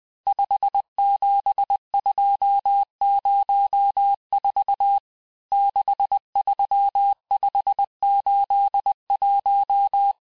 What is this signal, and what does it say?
Morse code, heard as a keyed tone, about 20 wpm, 800 Hz. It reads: 57204 63581
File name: cours-858-texte-chiffre-vitesse-19mpm